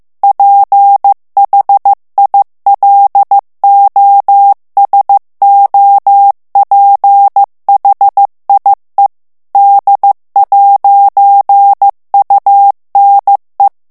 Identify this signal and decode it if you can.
Morse code, heard as a keyed tone, about 15 wpm, 800 Hz. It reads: PHILOSOPHIE D'UNE
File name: morse mp3 / FRANCAIS TEXTE CLAIR 2 VITESSE 14 MOTS MINUTES